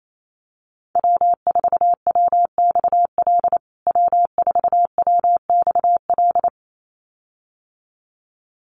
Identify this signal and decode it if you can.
Morse code, heard as a keyed tone, about 28 wpm, 700 Hz. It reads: W4WXL W4WXL
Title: USA Call Signs Repeated 2x 28wpm